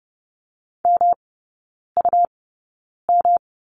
Morse code, heard as a keyed tone, about 30 wpm, spaced as for 9 wpm, 700 Hz. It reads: MUM
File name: Lesson 18e - Letter m - Three Characters 30wpm